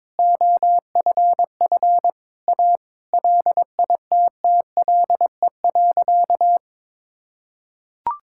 Morse code, heard as a keyed tone, about 22 wpm, 700 Hz. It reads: OFF A LITTLE.
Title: Sentences from Top 1000 Words - Review Part 6 - 22wpm